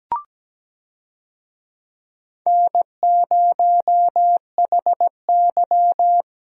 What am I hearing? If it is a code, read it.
Morse code, heard as a keyed tone, about 17 wpm, 700 Hz. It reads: N0HY
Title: USA Call Signs - 4 Characters - Repeated 3x - 17wpm